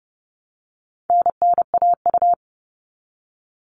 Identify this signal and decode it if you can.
Morse code, heard as a keyed tone, about 28 wpm, 700 Hz. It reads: NNAU
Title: All Cut Numbers - 4-Digit 30wpm